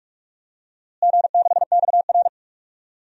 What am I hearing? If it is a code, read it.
Morse code, heard as a keyed tone, about 43 wpm, 700 Hz. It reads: G6XR
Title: DXCC-100 Call Signs - 4 Characters - Repeated 2x - 45wpm